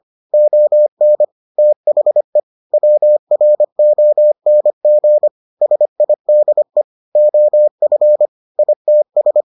Morse code, heard as a keyed tone, about 25 wpm, 600 Hz. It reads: ON THE WRONG SIDE OF ITS